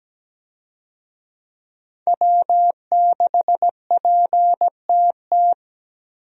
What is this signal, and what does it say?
Morse code, heard as a keyed tone, about 17 wpm, 700 Hz. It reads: W6PTT
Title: USA Call Signs - 5 Characters - 17wpm